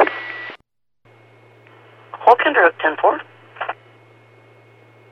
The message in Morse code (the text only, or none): none